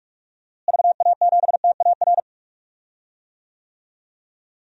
Morse code, heard as a keyed tone, about 42 wpm, 700 Hz. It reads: VA7TAR
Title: DXCC-100 Call Signs - 6 Characters - Repeated 3x - 45wpm